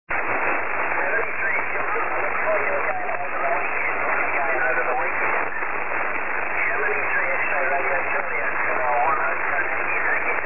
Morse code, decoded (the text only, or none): none